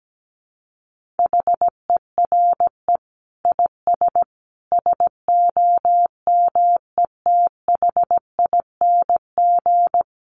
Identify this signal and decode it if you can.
Morse code, heard as a keyed tone, about 17 wpm, 700 Hz. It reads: HERE IS SOMETHING
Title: Sentences from Top 1000 Words 17wpm